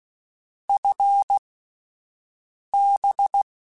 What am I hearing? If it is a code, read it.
Morse code, heard as a keyed tone, about 16 wpm, spaced as for 5 wpm, 800 Hz. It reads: FB
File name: QSOs / QSOs-mp3s / WD0BGP 1@13